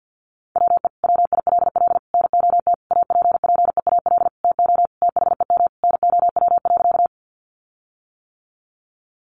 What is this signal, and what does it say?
Morse code, heard as a keyed tone, about 50 wpm, 700 Hz. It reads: WE WILL NOT APPEAL TO THEM NOW.
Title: Sentences from Top 1300 Words 50wpm